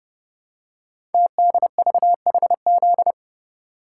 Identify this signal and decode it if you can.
Morse code, heard as a keyed tone, about 30 wpm, 700 Hz. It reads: TDVHZ
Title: Code Groups - 5 Letters - 30wpm